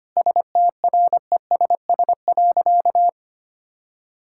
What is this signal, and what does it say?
Morse code, heard as a keyed tone, about 25 wpm, 700 Hz. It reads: STRESS.
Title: Sentences from Top 1000 Words - Review Part 6 - 25wpm